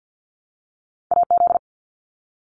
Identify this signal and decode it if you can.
Morse code, heard as a keyed tone, about 50 wpm, 700 Hz. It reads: AZ
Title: ARRL Sections 50wpm